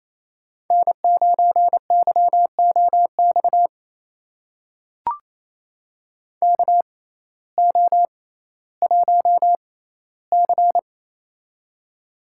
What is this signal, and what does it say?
Morse code, heard as a keyed tone, about 28 wpm, spaced as for 15 wpm, 700 Hz. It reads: N9YOX KO1C
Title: USA Call Signs - 6x Character Spacing - 28wpm